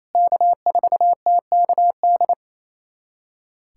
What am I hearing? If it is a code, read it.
Morse code, heard as a keyed tone, about 28 wpm, 700 Hz. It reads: K4TKD
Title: USA Callsign DE Callsign - 5 Characters - 28wpm